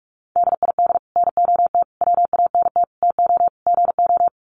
Morse code, heard as a keyed tone, about 44 wpm, 700 Hz. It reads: DID NOT WANT TO GO
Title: Sets of 5 Words 45wpm